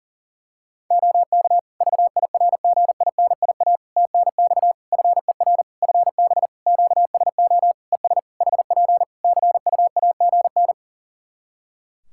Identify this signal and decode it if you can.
Morse code, heard as a keyed tone, about 40 wpm, 700 Hz. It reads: OK VIRGINIA TNX FER FB QSO ES HP CUAGN
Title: Practice QSO - Exchange 4 - Ending - 40wpm